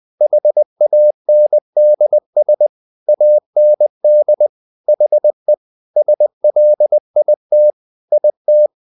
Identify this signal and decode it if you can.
Morse code, heard as a keyed tone, about 20 wpm, 600 Hz. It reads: HANDS AND HE SLIT IT
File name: the old man and the sea part 2 20wpm